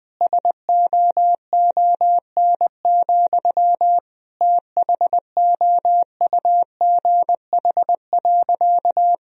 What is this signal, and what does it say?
Morse code, heard as a keyed tone, about 20 wpm, 700 Hz. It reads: SOON, THOUGH.
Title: Sentences from Top 1000 Words - Review Part 2 - 20wpm